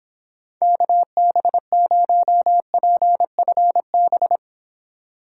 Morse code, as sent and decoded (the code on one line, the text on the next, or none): -.- -... ----- .--. ..-. -...
KB0PFB